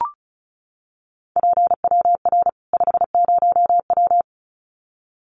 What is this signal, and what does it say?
Morse code, heard as a keyed tone, about 34 wpm, 700 Hz. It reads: PWR 50W
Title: Common QSO Phrases 35wpm